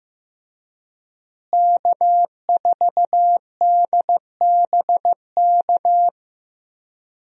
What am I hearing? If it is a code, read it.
Morse code, heard as a keyed tone, about 15 wpm, 700 Hz. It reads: K4DBK
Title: USA Call Signs - 5 Characters - Repeated 2x - 15wpm